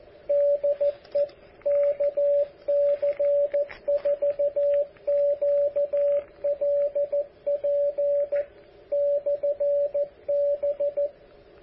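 Morse code, heard as a keyed tone, about 14 wpm, 570 Hz, with radio noise. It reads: DE KC4QLP/B
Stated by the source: KC4QLP 10 meter beacon, transmitter built by AA7DJ, power just over 10 watts.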